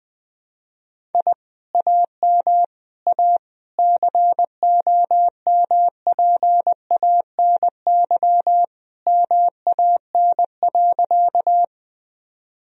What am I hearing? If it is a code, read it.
Morse code, heard as a keyed tone, about 20 wpm, 700 Hz. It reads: I AM A COMPANY MAN.
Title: Sentences from Top 1000 Words - Review Part 3 - 20wpm